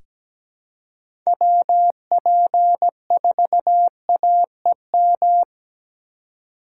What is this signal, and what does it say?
Morse code, heard as a keyed tone, about 17 wpm, 700 Hz. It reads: WP4AEM
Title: USA Call Signs - 2x Character Spacing - 17wpm